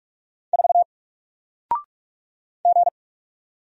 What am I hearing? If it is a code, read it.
Morse code, heard as a keyed tone, about 46 wpm, 700 Hz. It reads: 4 G
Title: Single Letter-Number 45wpm